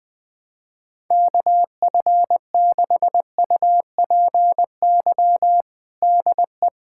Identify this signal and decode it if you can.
Morse code, heard as a keyed tone, about 21 wpm, 700 Hz. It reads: KF6UPY DE
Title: USA Callsign DE Callsign - 6 Characters - 20wpm